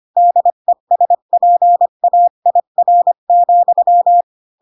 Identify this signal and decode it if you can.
Morse code, heard as a keyed tone, about 25 wpm, 700 Hz. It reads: DESPAIR,
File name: Pinocchio / Pinocchio ch7 20 25 ik1hge